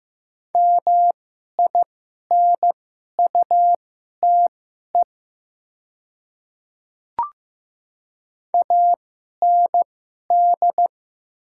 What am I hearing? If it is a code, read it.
Morse code, heard as a keyed tone, about 15 wpm, spaced as for 11 wpm, 700 Hz. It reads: MINUTE AND
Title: 2x Character Spacing - Top 500 Words - 15wpm